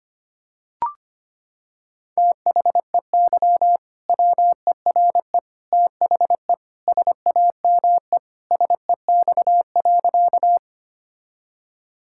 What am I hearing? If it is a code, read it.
Morse code, heard as a keyed tone, about 25 wpm, 700 Hz. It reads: THEY WERE THE SAME SEX.
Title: Sentences from Top 1600 Words 25wpm